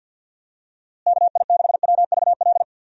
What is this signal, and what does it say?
Morse code, heard as a keyed tone, about 51 wpm, 700 Hz. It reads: KI6WVL
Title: USA Call Signs - 6 Characters - Repeated 3x - 50wpm